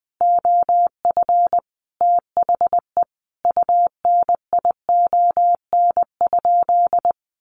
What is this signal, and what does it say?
Morse code, heard as a keyed tone, about 20 wpm, 700 Hz. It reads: OF THE UNION?
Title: Sentences from Top 1400 Words 20wpm